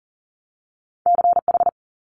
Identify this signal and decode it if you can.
Morse code, heard as a keyed tone, about 40 wpm, 700 Hz. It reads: CH
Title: Multi-letter Phonemes 40wpm